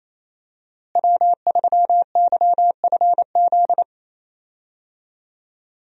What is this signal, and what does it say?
Morse code, heard as a keyed tone, about 28 wpm, 700 Hz. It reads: W3YFZ
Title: USA Call Signs - 5 Characters - Repeated 3x - 28wpm